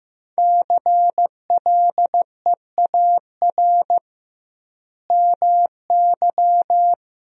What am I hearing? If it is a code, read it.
Morse code, heard as a keyed tone, about 16 wpm, 700 Hz. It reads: CLEAR MY
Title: 2x Word-Spacing - Sentences from Top 500 Words - 15wpm